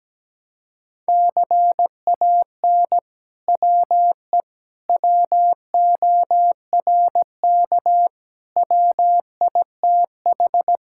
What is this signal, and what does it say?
Morse code, heard as a keyed tone, about 17 wpm, 700 Hz. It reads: CAN WE WORK WITH
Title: Sentences from Top 1000 Words - Review Part 5 - 17wpm